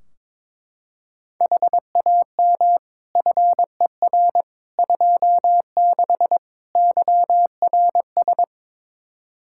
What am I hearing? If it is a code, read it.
Morse code, heard as a keyed tone, about 22 wpm, 700 Hz. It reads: HAM FER 26 YRS
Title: Practice QSO - Exchange 3 - More Info - 22wpm